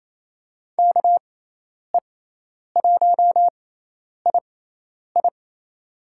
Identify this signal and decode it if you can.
Morse code, heard as a keyed tone, about 28 wpm, spaced as for 10 wpm, 700 Hz. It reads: KE1II
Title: USA Call Signs - 6x Character Spacing - 28wpm